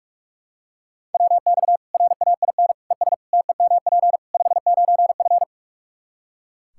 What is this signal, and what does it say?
Morse code, heard as a keyed tone, about 44 wpm, 700 Hz. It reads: WX RAIN ES TEMP 59F
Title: Practice QSO - Exchange 2 - Info - 45wpm